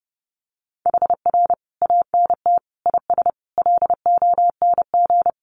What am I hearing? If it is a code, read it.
Morse code, heard as a keyed tone, about 30 wpm, 700 Hz. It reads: HR ANT IS LONG
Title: Common QSO Phrases 30wpm